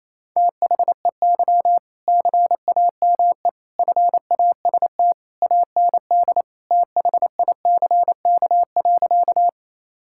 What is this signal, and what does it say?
Morse code, heard as a keyed tone, about 28 wpm, 700 Hz. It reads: THEY CAME FAST AND THICK.
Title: Sentences from Top 1600 Words 28wpm